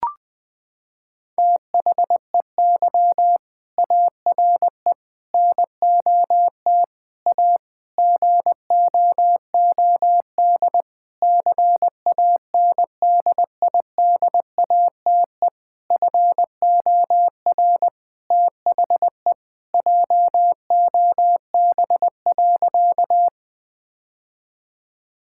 Morse code, heard as a keyed tone, about 20 wpm, 700 Hz. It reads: THEY ARE NOT A GOOD CANDIDATE FOR THE JOB.
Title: Sentences from Top 1600 Words 20wpm